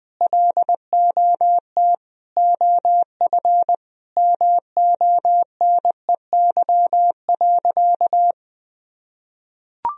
Morse code, heard as a keyed tone, about 21 wpm, 700 Hz. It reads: LOT OF MONEY.
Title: Sentences from Top 1000 Words - Review Part 4 - 20wpm